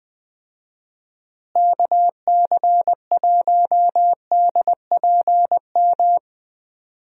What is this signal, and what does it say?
Morse code, heard as a keyed tone, about 20 wpm, 700 Hz. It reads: KC1DPM
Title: USA Call Signs - 6 Characters - Repeated 3x - 20wpm